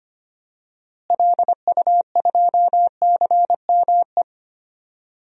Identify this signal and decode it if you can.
Morse code, heard as a keyed tone, about 25 wpm, 700 Hz. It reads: LU2CME
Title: DXCC-100 Call Signs - 6 Characters - Repeated 3x - 25wpm